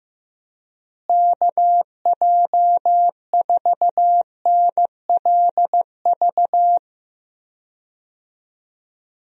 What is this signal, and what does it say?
Morse code, heard as a keyed tone, about 16 wpm, 700 Hz. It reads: KJ4NLV
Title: USA Call Signs - 6 Characters - Repeated 2x - 15wpm